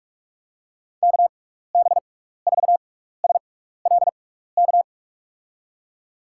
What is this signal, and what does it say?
Morse code, heard as a keyed tone, about 41 wpm, spaced as for 15 wpm, 700 Hz. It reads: KB4SLK
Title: USA Call Signs - 6x Character Spacing - 45wpm